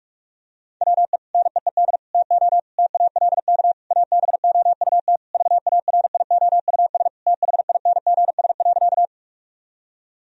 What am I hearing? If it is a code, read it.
Morse code, heard as a keyed tone, about 46 wpm, 700 Hz. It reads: WE NEED TO TALK ABOUT VARIOUS THINGS.
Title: Sentences from Top 1000 Words - Review Part 4 - 45wpm